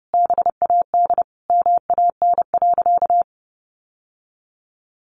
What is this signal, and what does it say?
Morse code, heard as a keyed tone, about 30 wpm, 700 Hz. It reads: BAD MAN.
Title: Sentences from Top 1200 Words 30wpm